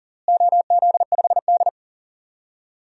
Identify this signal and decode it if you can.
Morse code, heard as a keyed tone, about 40 wpm, 700 Hz. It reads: OZ5D
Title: DXCC-100 Call Signs - 4 Characters - Repeated 2x - 40wpm